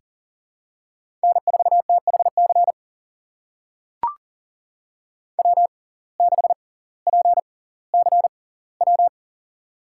Morse code, heard as a keyed tone, about 40 wpm, spaced as for 20 wpm, 700 Hz. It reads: N4THC W6PCW
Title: USA Call Signs - 5 Characters - 6x Character Spacing - 40wpm